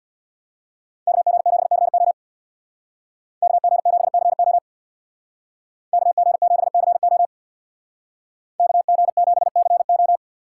Morse code, heard as a keyed tone, about 57 wpm, 700 Hz. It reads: KG7CO KG7CO KG7CO KG7CO
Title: Speed Racing - USA 2x2 Call Signs 50wpm